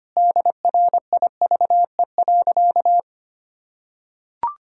Morse code, heard as a keyed tone, about 25 wpm, 700 Hz. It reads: DRIVE.
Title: Sentences from Top 1000 Words - Review Part 2 - 25wpm